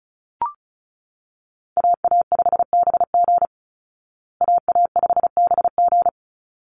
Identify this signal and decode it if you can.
Morse code, heard as a keyed tone, about 34 wpm, 700 Hz. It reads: AA5BG AA5BG
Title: Call Signs Repeated 3x - 35wpm